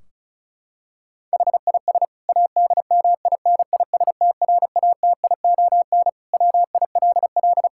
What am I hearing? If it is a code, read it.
Morse code, heard as a keyed tone, about 36 wpm, 690 Hz. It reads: HIS ADMINISTRATION WILL
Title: Sentences from Top 1600 Words 35wpm